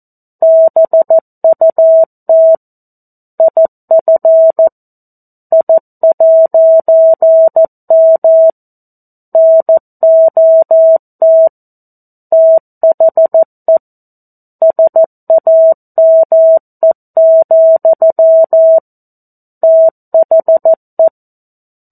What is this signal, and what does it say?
Morse code, heard as a keyed tone, about 14 wpm, 650 Hz. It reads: BUT IF I'M NOT THE SAME, THE
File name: 02-Chapter-02 14 WPM